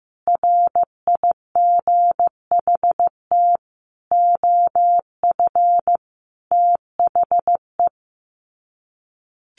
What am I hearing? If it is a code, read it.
Morse code, heard as a keyed tone, about 15 wpm, 700 Hz. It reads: RIGHT OF THE
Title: Sets of 5 Words Encore 15wpm